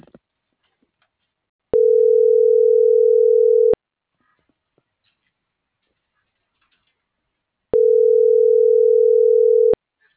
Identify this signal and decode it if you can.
Ringback tone (North America), 440 + 480 Hz.